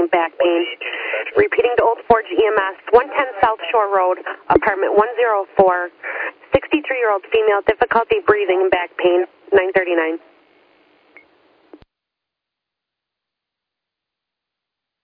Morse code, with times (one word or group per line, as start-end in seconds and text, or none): none